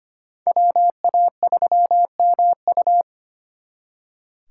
Morse code, heard as a keyed tone, about 32 wpm, 700 Hz. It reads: WA3MU